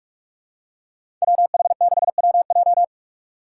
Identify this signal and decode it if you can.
Morse code, heard as a keyed tone, about 46 wpm, 700 Hz. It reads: WH6WJ
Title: USA Call Signs - 2x Character Spacing - 45wpm